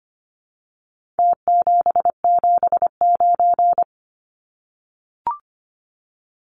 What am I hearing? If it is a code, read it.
Morse code, heard as a keyed tone, about 25 wpm, 700 Hz. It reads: T779
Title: Numbers - Contest Style - 4-Digit 25wpm